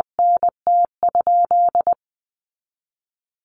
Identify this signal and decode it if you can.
Morse code, heard as a keyed tone, about 20 wpm, 700 Hz. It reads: NT?
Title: Sentences from Top 1500 Words 20wpm